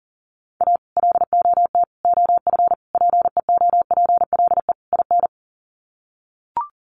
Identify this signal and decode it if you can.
Morse code, heard as a keyed tone, about 40 wpm, 700 Hz. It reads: A LOT OF PEOPLE IN
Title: Sets of 5 Words 40wpm